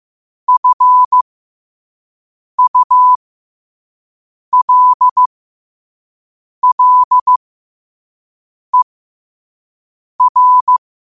Morse code, heard as a keyed tone, about 15 wpm, spaced as for 5 wpm, 1000 Hz. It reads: FULLER
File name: qotdfiles / 5wpm